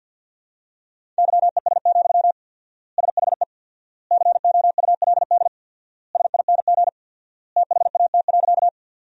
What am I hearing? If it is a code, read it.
Morse code, heard as a keyed tone, about 50 wpm, 700 Hz. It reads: YES, SHE COULD SING THAT.
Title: 4x Word-Spacing - Sentences from Top 500 Words - 50wpm